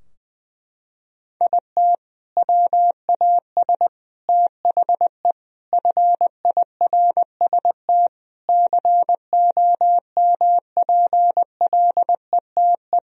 Morse code, heard as a keyed tone, about 21 wpm, 700 Hz. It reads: IT WAS THE FIRST COMPLETE
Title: Sentences from Top 1000 Words - Review Part 2 - 20wpm